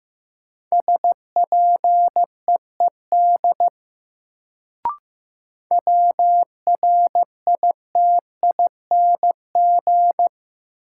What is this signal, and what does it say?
Morse code, heard as a keyed tone, about 15 wpm, 700 Hz. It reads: SPEED WRITING
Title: Sentences from Top 800 Words - New Word Drill - 15wpm